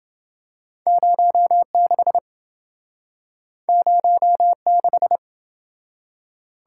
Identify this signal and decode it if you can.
Morse code, heard as a keyed tone, about 28 wpm, 700 Hz. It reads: 06 06